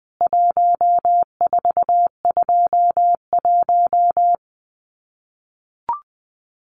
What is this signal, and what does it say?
Morse code, heard as a keyed tone, about 20 wpm, 700 Hz. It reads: 1421
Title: CWT Exchange 20wpm